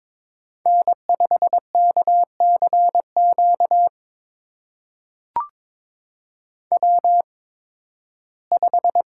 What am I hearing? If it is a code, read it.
Morse code, heard as a keyed tone, about 22 wpm, spaced as for 14 wpm, 700 Hz. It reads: N5KCQ W5
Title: USA Call Signs - 5 Characters - 8x Character Spacing - 22wpm